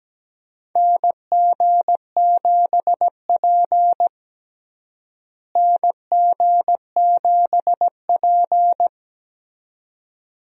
Morse code, heard as a keyed tone, about 17 wpm, 700 Hz. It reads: NG7P NG7P
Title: USA Call Signs - 4 Characters - Repeated 2x - 17wpm